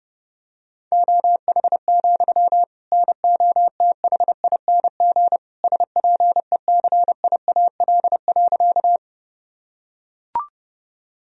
Morse code, heard as a keyed tone, about 30 wpm, 700 Hz. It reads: OH, NOTHING SPECIAL.
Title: Sentences from Top 1000 Words - Review Part 3 - 30wpm